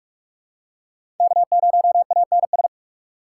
Morse code, heard as a keyed tone, about 41 wpm, 700 Hz. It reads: K0ANS
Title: USA Call Signs - 5 Characters - 2x Character Spacing - 45wpm